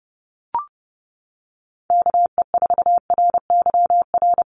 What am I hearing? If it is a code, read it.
Morse code, heard as a keyed tone, about 30 wpm, 700 Hz. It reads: KE4RYR
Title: USA Call Signs Repeated 3x 30wpm